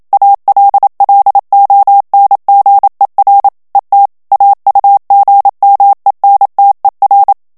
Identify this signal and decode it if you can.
Morse code, heard as a keyed tone, about 27 wpm, 800 Hz. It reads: ALLONGER ET AUGMENTER
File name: CW / FICHIERS-MP3 / FRANCAIS TEXTE CLAIR 1 VITESSE 26 MOTS MINUTES